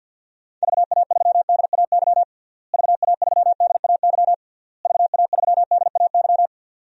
Morse code, heard as a keyed tone, about 52 wpm, 700 Hz. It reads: VA3DAY VA3DAY VA3DAY
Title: DXCC-100 Call Signs - 6 Characters - Repeated 3x - 50wpm